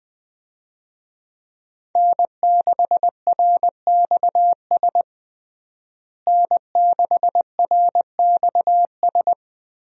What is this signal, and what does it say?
Morse code, heard as a keyed tone, about 20 wpm, 700 Hz. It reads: N6RXS N6RXS
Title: USA Call Signs - 5 Characters - Repeated 3x - 20wpm